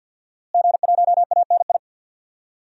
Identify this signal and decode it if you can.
Morse code, heard as a keyed tone, about 52 wpm, 700 Hz. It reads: G1ANI
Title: DXCC-100 Call Signs - 5 Characters - Repeated 2x - 50wpm